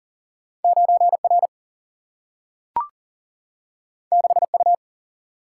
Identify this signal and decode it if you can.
Morse code, heard as a keyed tone, about 40 wpm, 700 Hz. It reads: 9 R 6 U
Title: Single-Letter Single-Number 40wpm